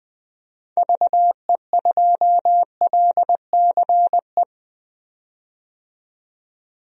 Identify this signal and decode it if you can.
Morse code, heard as a keyed tone, about 20 wpm, 700 Hz. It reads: VE2LCE
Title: DXCC-100 Call Signs - 6 Characters - 20wpm